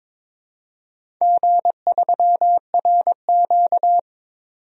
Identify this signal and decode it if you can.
Morse code, heard as a keyed tone, about 22 wpm, 700 Hz. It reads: G3RQ
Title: DXCC-100 Call Signs - 4 Characters - 22wpm